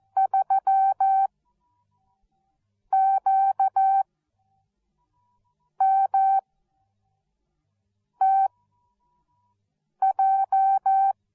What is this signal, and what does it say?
Morse code, heard as a keyed tone, about 14 wpm, spaced as for 4 wpm, 770 Hz. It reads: 3QMTJ
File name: Onlinelehrgang / morsen / DJ4UF Morsen13